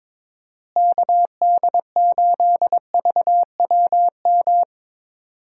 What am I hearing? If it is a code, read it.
Morse code, heard as a keyed tone, about 22 wpm, 700 Hz. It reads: KD8VWM
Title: USA Call Signs - 6 Characters - Repeated 2x - 22wpm